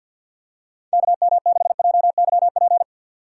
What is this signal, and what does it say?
Morse code, heard as a keyed tone, about 47 wpm, 700 Hz. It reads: KM6JYP
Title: USA Call Signs - 6 Characters - 3x Character Spacing - 50wpm